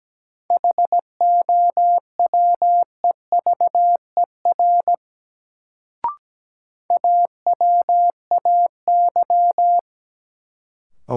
Morse code, heard as a keyed tone, about 17 wpm, 700 Hz. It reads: HOWEVER AWAY
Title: Sentences from Top 600 Words - New Word Drill - 17wpm